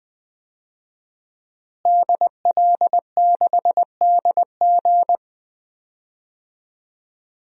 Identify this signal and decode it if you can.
Morse code, heard as a keyed tone, about 20 wpm, 700 Hz. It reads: DL6DG
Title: DXCC-100 Call Signs - 5 Characters - Repeated 3x - 20wpm